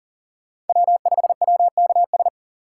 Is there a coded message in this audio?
Yes: Morse code from 0.7 to 2.3 s, heard as a keyed tone, about 51 wpm, 700 Hz. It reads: W5WKS